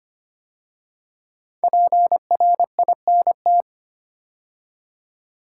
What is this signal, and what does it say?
Morse code, heard as a keyed tone, about 25 wpm, 700 Hz. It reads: PRINT
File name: 5 Letter Words - First 3 Letters Given - 25wpm